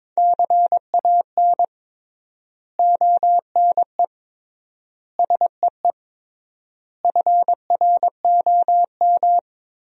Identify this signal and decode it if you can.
Morse code, heard as a keyed tone, about 22 wpm, 700 Hz. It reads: CAN ONE SEE FROM
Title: 3x Word-Spacing - Sentences from Top 100 Words - 22wpm